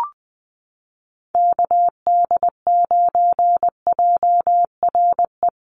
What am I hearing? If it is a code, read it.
Morse code, heard as a keyed tone, about 20 wpm, 700 Hz. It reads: KD9JRE
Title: USA Call Signs Repeated 2x 20wpm